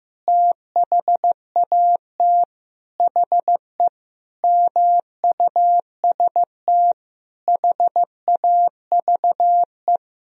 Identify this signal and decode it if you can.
Morse code, heard as a keyed tone, about 15 wpm, 700 Hz. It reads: THAT HE MUST HAVE
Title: Sentences from Top 1000 Words - Review Part 1 - 15wpm